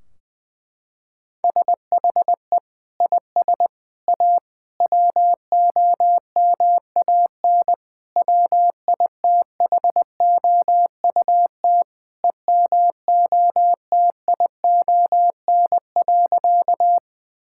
Morse code, heard as a keyed tone, about 20 wpm, 700 Hz. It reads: SHE IS A WOMAN WITHOUT EMOTION.
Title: Sentences from Top 1600 Words 20wpm